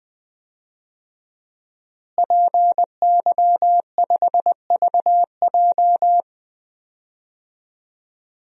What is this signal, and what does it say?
Morse code, heard as a keyed tone, about 20 wpm, 700 Hz. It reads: PY5VJ
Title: DXCC-100 Call Signs - 5 Characters - Repeated 3x - 20wpm